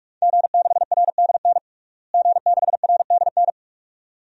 Morse code, heard as a keyed tone, about 45 wpm, 700 Hz. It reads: G6RDN G6RDN
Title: DXCC-100 Call Signs - 5 Characters - Repeated 2x - 45wpm